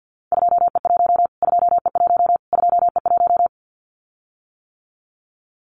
Morse code, heard as a keyed tone, about 50 wpm, 700 Hz. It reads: 2E1 2E1 2E1
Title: Callsign Prefixes for Top 25 DXCC - Repeated 3x - 50wpm